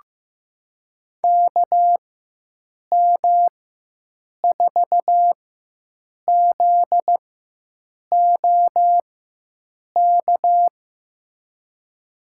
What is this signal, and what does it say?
Morse code, heard as a keyed tone, about 15 wpm, spaced as for 7 wpm, 700 Hz. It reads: KM4ZOK
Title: USA Call Signs - 4x Character Spacing - 15wpm